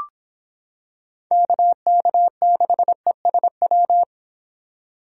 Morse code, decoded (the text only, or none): KK6ESW